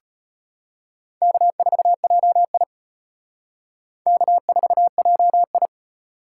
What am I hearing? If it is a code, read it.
Morse code, heard as a keyed tone, about 31 wpm, 700 Hz. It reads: K4JI K4JI
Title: Speed Racing - USA 1x2 Call Signs 25wpm